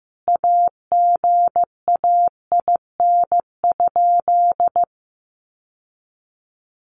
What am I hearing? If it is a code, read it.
Morse code, heard as a keyed tone, about 15 wpm, 700 Hz. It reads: AGAIN?
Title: Sentences from Top 1200 Words 15wpm